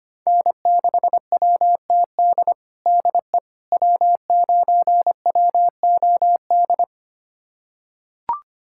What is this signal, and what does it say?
Morse code, heard as a keyed tone, about 25 wpm, 700 Hz. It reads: N6WTD DE W9WOD
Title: USA Callsign DE Callsign - 5 Characters - 25wpm